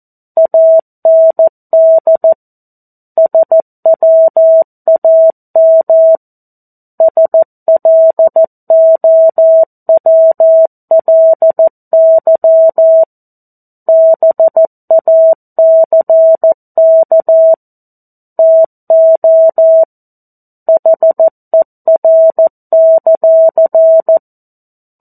Morse code, heard as a keyed tone, about 14 wpm, 650 Hz. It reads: AND SWAM SLOWLY BACK TO HER;
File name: 02-Chapter-02 14 WPM